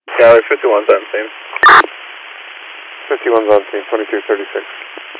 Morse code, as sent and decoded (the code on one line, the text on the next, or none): none